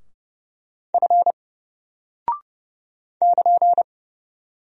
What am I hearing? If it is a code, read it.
Morse code, heard as a keyed tone, about 30 wpm, 700 Hz. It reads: F <KN>
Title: Single Character (Letter:Number:Punctuation:Prosign) 30wpm